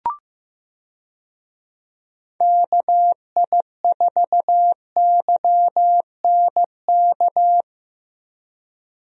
Morse code, heard as a keyed tone, about 15 wpm, 700 Hz. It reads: KI4YNK
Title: USA Call Signs - 6 Characters - Repeated 3x - 15wpm